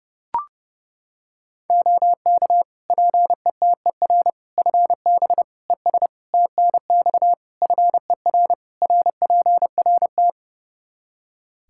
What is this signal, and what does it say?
Morse code, heard as a keyed tone, about 30 wpm, 700 Hz. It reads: OK PETER FB ES TNX FER RPRT
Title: K1USN SST QSO Practice 30wpm